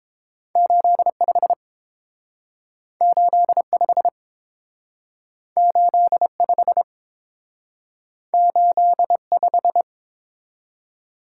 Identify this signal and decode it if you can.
Morse code, heard as a keyed tone, about 29 wpm, 700 Hz. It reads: 85 85 85 85